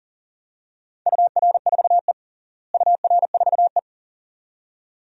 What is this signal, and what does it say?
Morse code, heard as a keyed tone, about 40 wpm, 700 Hz. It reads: UR4E UR4E
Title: DXCC-100 Call Signs - 4 Characters - Repeated 2x - 40wpm